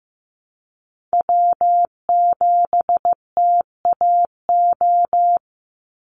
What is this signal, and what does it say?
Morse code, heard as a keyed tone, about 15 wpm, 700 Hz. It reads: W7TAO
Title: NA SOTA Chaser Callsigns 15wpm